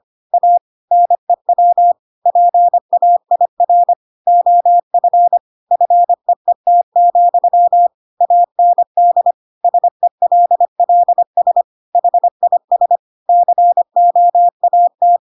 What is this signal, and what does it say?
Morse code, heard as a keyed tone, about 25 wpm, 700 Hz. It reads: A NEW PAIR OF FEET, AND SELLS HIS COAT
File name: Pinocchio / Pinocchio ch8 25 25 ik1hge